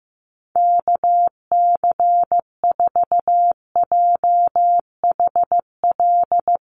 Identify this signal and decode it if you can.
Morse code, heard as a keyed tone, about 14 wpm, 700 Hz. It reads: KC4JHL
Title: USA Call Signs Repeated 2x 15wpm